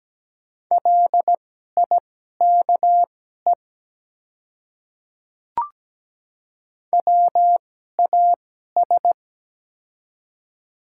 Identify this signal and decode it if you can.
Morse code, heard as a keyed tone, about 17 wpm, spaced as for 12 wpm, 700 Hz. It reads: LIKE WAS
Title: 2x Character Spacing - Top 100 Words - 17wpm